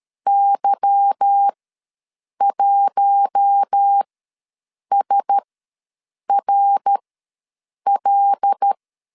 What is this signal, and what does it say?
Morse code, heard as a keyed tone, about 13 wpm, spaced as for 7 wpm, 800 Hz. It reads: Y1SRL
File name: LEZIONI / 9 2867 ws 12wpm 3431